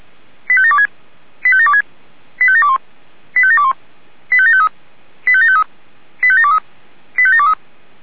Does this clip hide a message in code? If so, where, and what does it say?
Yes: five-tone selective call (ZVEI) from 0.5 to 7.5 s, heard as quick runs of five tones, tones of about 70 ms. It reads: 85726 85726 85721 85721 85753 85753 85723 85723